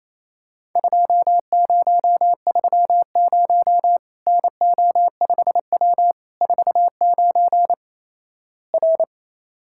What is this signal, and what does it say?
Morse code, heard as a keyed tone, about 28 wpm, 700 Hz. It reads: 2030 NO5W 49 R
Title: Worked All Europe DX Contest - QTC Exchange - 28wpm